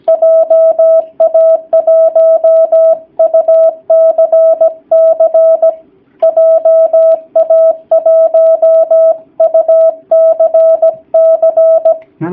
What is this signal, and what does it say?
Morse code, heard as a keyed tone, about 17 wpm, 650 Hz. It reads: JA1UCC JA1UCC
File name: jh8cbh / 2019cw / sp16- w